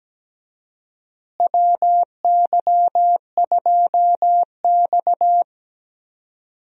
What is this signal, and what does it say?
Morse code, heard as a keyed tone, about 17 wpm, 700 Hz. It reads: WY2X
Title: USA Call Signs - 4 Characters - Repeated 2x - 17wpm